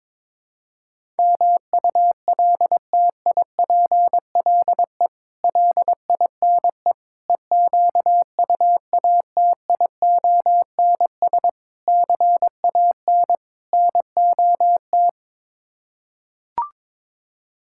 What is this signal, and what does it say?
Morse code, heard as a keyed tone, about 22 wpm, 700 Hz. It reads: MULTIPLE LINE EQUATIONS CAN NOT
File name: practice / 22wpm / Build A Phrase - 5-Word Phrases - 22wpm